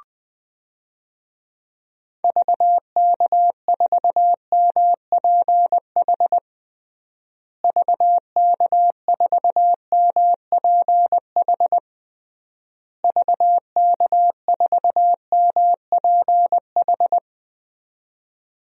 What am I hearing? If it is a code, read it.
Morse code, heard as a keyed tone, about 20 wpm, 700 Hz. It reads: VK4MPH VK4MPH VK4MPH
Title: DXCC-100 Call Signs - 6 Characters - Repeated 3x - 20wpm